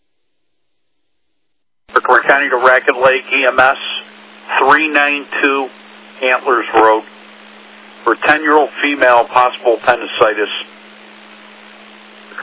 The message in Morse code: none

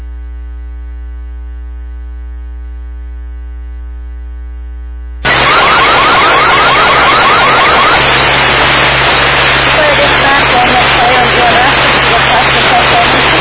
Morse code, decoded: none

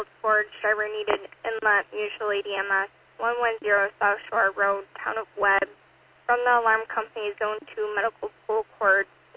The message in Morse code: none

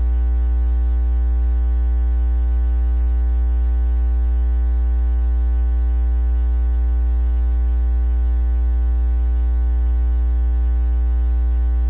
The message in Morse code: none

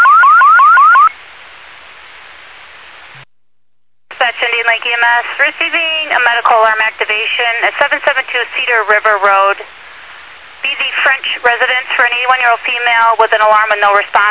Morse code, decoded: none

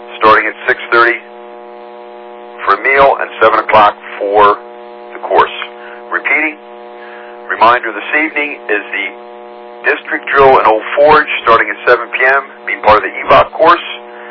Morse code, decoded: none

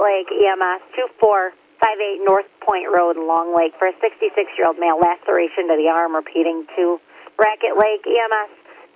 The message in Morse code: none

